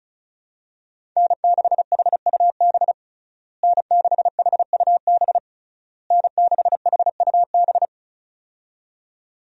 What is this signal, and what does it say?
Morse code, heard as a keyed tone, about 36 wpm, 700 Hz. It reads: N6HUB N6HUB N6HUB
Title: USA Call Signs - 5 Characters - Repeated 3x - 35wpm